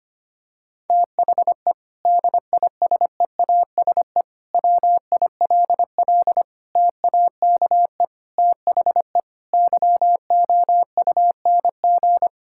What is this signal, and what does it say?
Morse code, heard as a keyed tone, about 25 wpm, 700 Hz. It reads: THE DISEASE WILL TAKE THE YOUNG
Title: Sentences from Top 1000 Words - Review Part 6 - 25wpm